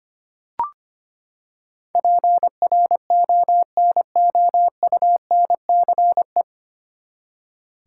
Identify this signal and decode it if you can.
Morse code, heard as a keyed tone, about 25 wpm, 700 Hz. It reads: PRONOUNCE
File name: Nine Letter Words - 25wpm